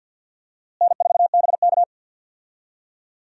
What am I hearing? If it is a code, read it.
Morse code, heard as a keyed tone, about 52 wpm, 700 Hz. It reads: N4BK
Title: USA Call Signs - 4 Characters - Repeated 2x - 50wpm